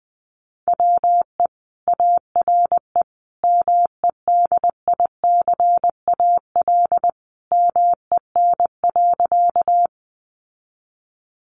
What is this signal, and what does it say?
Morse code, heard as a keyed tone, about 20 wpm, 700 Hz. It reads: WE ARE MEDICAL MEN.
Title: Sentences from Top 800 Words 20wpm